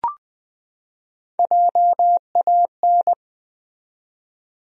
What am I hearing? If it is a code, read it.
Morse code, heard as a keyed tone, about 20 wpm, 700 Hz. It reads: JAN
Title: USA Common Names - Max 3 Letters - 20wpm